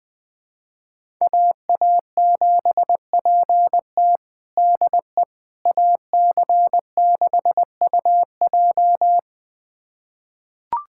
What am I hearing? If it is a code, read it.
Morse code, heard as a keyed tone, about 20 wpm, 700 Hz. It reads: AA7PT DE AC6UJ
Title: USA Callsign DE Callsign - 5 Characters - 20wpm